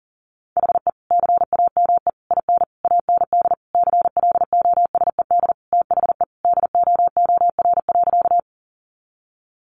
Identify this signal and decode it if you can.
Morse code, heard as a keyed tone, about 40 wpm, 700 Hz. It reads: HE CAME IN AND CLOSED THE DOOR.
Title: Sentences from Top 1400 Words 40wpm